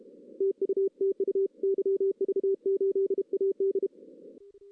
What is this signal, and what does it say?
Morse code, heard as a keyed tone, about 32 wpm, 380 Hz. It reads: TU TU YV8AD
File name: 2016 arrl dx cw / 2016-ARRL-DX-CW-K5TR- - 0-3